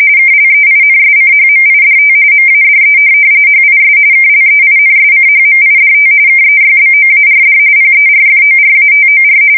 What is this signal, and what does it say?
RTTY, about 45.45 baud. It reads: KLEINSCHMIDT MADE HIS KLEINSCHMIDT KEYBOARD PERFORATOR
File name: fsk170 / teletype-corp